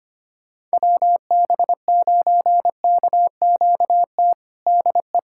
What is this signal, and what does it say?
Morse code, heard as a keyed tone, about 25 wpm, 700 Hz. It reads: WB9KQT DE
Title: USA Callsign DE Callsign - 6 Characters - 25wpm